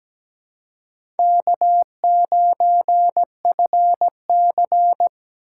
Morse code, heard as a keyed tone, about 17 wpm, 700 Hz. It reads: K9FC
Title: USA Call Signs - 4 Characters - 8x Character Spacing - 17wpm